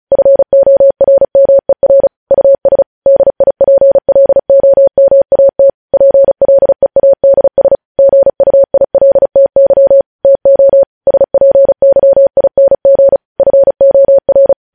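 Morse code, heard as a keyed tone, about 36 wpm, 550 Hz. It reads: FORMER US DIPLOMAT PLEADS GUILTY TO SPYING FOR
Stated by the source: News headlines sent in Morse at speed 35.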